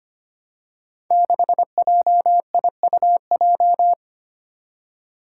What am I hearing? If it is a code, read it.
Morse code, heard as a keyed tone, about 25 wpm, 700 Hz. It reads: 6JIUJ
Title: Five Challenging Letter-Number Combinations 25wpm